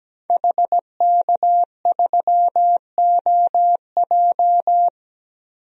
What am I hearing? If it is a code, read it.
Morse code, heard as a keyed tone, about 17 wpm, 700 Hz. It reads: HK3OJ
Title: DXCC-100 Call Signs - 5 Characters - 17wpm